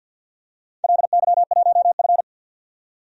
Morse code, heard as a keyed tone, about 47 wpm, 700 Hz. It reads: LY1F
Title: DXCC-100 Call Signs - 4 Characters - 50wpm